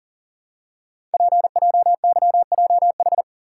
Morse code, heard as a keyed tone, about 42 wpm, 700 Hz. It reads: PJYJH
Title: Code Groups - 5 Letters - 40wpm